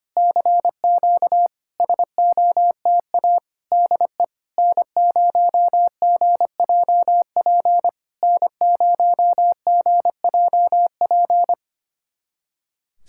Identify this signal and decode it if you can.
Morse code, heard as a keyed tone, about 25 wpm, 690 Hz. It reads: CQ SOTA DE N0GJP N0GJP
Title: SOTA QSO Practice 25wpm